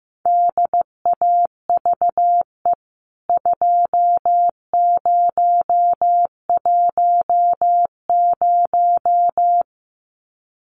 Morse code, heard as a keyed tone, about 15 wpm, 700 Hz. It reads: DAVE 2010
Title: CWT Exchange 15wpm